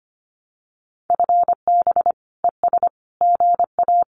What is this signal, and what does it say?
Morse code, heard as a keyed tone, about 25 wpm, 700 Hz. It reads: FB ES GA
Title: Common QSO Phrases Encore 25wpm